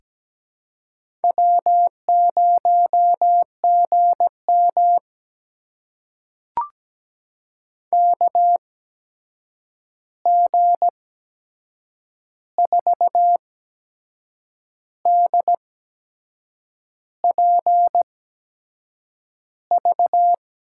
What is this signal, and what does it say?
Morse code, heard as a keyed tone, about 17 wpm, spaced as for 6 wpm, 700 Hz. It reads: W0GM KG4DPV
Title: USA Call Signs - 8x Character Spacing - 17wpm